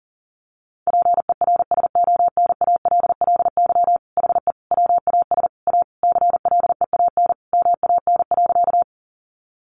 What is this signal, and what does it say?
Morse code, heard as a keyed tone, about 40 wpm, 700 Hz. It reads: PERSONALLY HE WAS A CLEAN MAN.
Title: Sentences from Top 1400 Words 40wpm